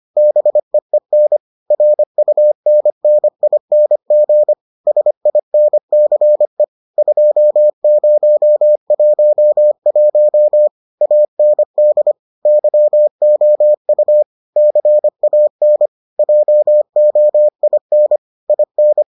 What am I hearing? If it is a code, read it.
Morse code, heard as a keyed tone, about 25 wpm, 600 Hz. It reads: BEEN RUNNING SINCE 2011 AND YOU CAN JOIN IN